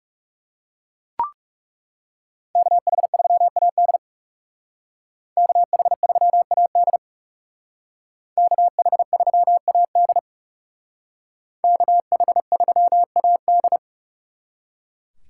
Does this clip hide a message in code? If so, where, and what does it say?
Yes: Morse code from 2.6 to 13.8 s, heard as a keyed tone, about 39 wpm, 700 Hz. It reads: KH3AD KH3AD KH3AD KH3AD